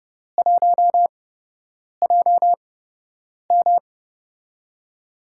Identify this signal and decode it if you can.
Morse code, heard as a keyed tone, about 31 wpm, spaced as for 8 wpm, 700 Hz. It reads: 1JM
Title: USA Call Signs - 4 Characters - 8x Character Spacing - 30wpm